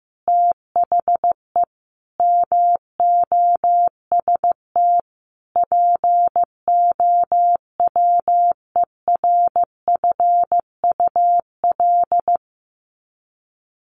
Morse code, heard as a keyed tone, about 15 wpm, 700 Hz. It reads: THE MOST POWERFUL
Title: Sets of 5 Words Encore 15wpm